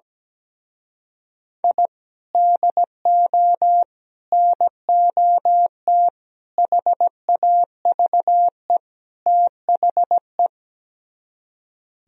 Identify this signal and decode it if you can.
Morse code, heard as a keyed tone, about 17 wpm, 700 Hz. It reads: I DO NOT HAVE THE
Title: Speed-Racing - Sets of 5 Words 17wpm